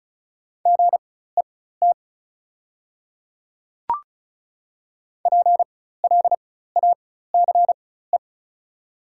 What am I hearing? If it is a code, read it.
Morse code, heard as a keyed tone, about 36 wpm, spaced as for 16 wpm, 690 Hz. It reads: GET PLACE
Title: 4x Character Spacing - Top 200 Words - 35wpm